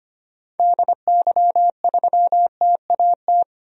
Morse code, heard as a keyed tone, about 25 wpm, 700 Hz. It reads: DY3TAT
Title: DXCC-100 Call Signs - 6 Characters - 25wpm